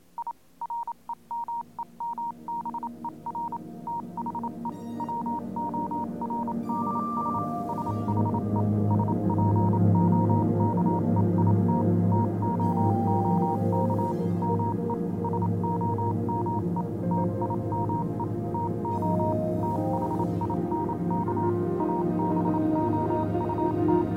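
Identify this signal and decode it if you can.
Morse code, heard as a keyed tone, about 27 wpm, 940 Hz, with noise. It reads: I REMEMBER THE WORDS SHE SPOKE IT TOOK ME SOME TIME TO THINK ABOUT
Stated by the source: Morse code version of a fan-made pony music track.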